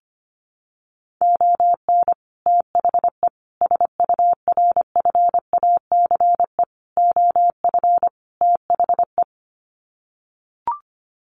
Morse code, heard as a keyed tone, about 25 wpm, 700 Hz. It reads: ON THE SURFACE OF THE
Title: Sets of 5 Words Encore 25wpm